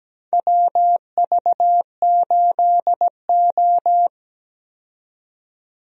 Morse code, heard as a keyed tone, about 17 wpm, 700 Hz. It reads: WV8O
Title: USA Call Signs - 4 Characters - 17wpm